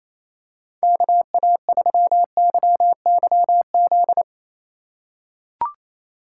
Morse code, heard as a keyed tone, about 28 wpm, 700 Hz. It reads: KA3YYZ
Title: USA Call Signs - 3x Character Spacing - 28wpm